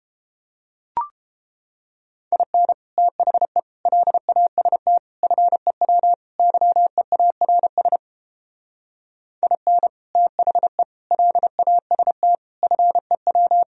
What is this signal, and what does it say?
Morse code, heard as a keyed tone, about 31 wpm, 700 Hz. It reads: IN THE LAST FEW YEARS IN THE LAST FEW